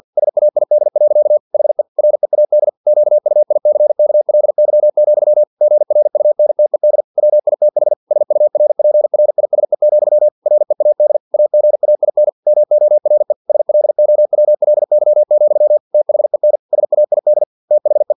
Morse code, heard as a keyed tone, about 50 wpm, 600 Hz. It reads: SAID. HE READ QUICKLY, GRUNTED WITH SURPRISE, READ AGAIN MORE SLOWLY, THEN SAID THE
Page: Never 49wpm